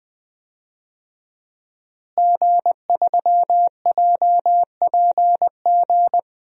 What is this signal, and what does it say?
Morse code, heard as a keyed tone, about 21 wpm, 700 Hz. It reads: G3JPG
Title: DXCC-100 Call Signs - 5 Characters - Repeated 2x - 20wpm